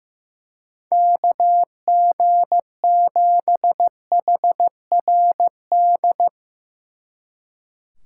Morse code, heard as a keyed tone, about 16 wpm, 700 Hz. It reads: KG7HRD
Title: Speed Racing - USA Call Signs 15wpm